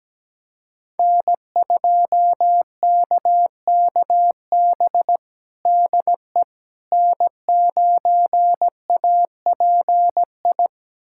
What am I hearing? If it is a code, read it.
Morse code, heard as a keyed tone, about 17 wpm, 700 Hz. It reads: N2KKB DE N9API
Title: USA Callsign DE Callsign - 5 Characters - 17wpm